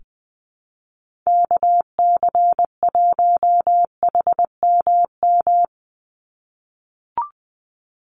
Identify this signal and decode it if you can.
Morse code, heard as a keyed tone, about 20 wpm, 700 Hz. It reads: KC1HMM
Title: USA Call Signs 20wpm